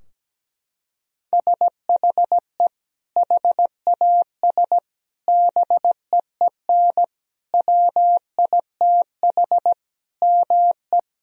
Morse code, heard as a keyed tone, about 17 wpm, 700 Hz. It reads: SHE HAS BEEN WITH ME
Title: Sentences from Top 1000 Words - Review Part 4 - 17wpm